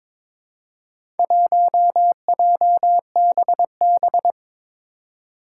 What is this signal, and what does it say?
Morse code, heard as a keyed tone, about 22 wpm, 700 Hz. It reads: 1JBB
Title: Four Challenging Letter-Number Combinations 22wpm